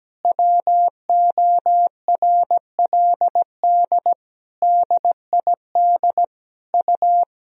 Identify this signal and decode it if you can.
Morse code, heard as a keyed tone, about 17 wpm, 700 Hz. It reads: WORLD DID U
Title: Sentences from Top 200 Words and Abbreviations 17wpm